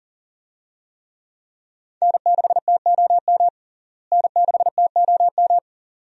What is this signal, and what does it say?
Morse code, heard as a keyed tone, about 40 wpm, 700 Hz. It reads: N6TOM N6TOM
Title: USA Call Signs - 5 Characters - Repeated 2x - 40wpm